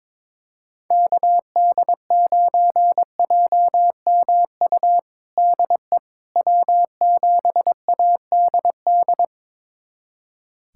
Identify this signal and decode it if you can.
Morse code, heard as a keyed tone, about 21 wpm, 700 Hz. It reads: KD9JMU DE W7ADD
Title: USA Callsign DE Callsign 22wpm